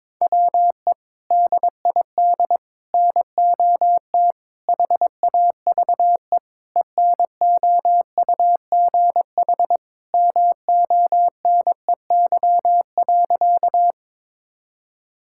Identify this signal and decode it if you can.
Morse code, heard as a keyed tone, about 21 wpm, 700 Hz. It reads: WE DID NOT HAVE ENOUGH MONEY.
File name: practice / 22wpm / Sentences with 4-Word Phrases - 22wpm